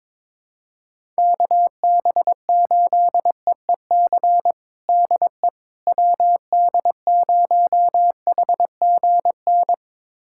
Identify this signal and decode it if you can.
Morse code, heard as a keyed tone, about 22 wpm, 700 Hz. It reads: KB8EEC DE WD0HGN
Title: USA Callsign DE Callsign - 6 Characters - 22wpm